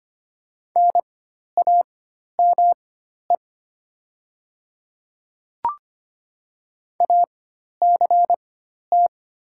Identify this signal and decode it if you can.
Morse code, heard as a keyed tone, about 25 wpm, spaced as for 12 wpm, 700 Hz. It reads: NAME ACT
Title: 4x Character Spacing - Top 200 Words - 25wpm